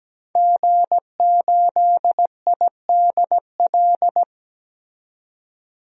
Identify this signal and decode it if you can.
Morse code, heard as a keyed tone, about 17 wpm, 700 Hz. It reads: G8IDL
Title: DXCC-100 Call Signs - 5 Characters - 17wpm